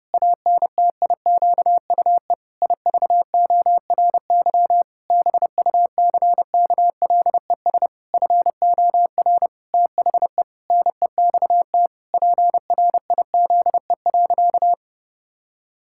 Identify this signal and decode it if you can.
Morse code, heard as a keyed tone, about 32 wpm, 700 Hz. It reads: ANTIQUE IVORY BUCKLES FOR THE NEXT PRIZE.
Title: Pangram Challenge v2 - 30wpm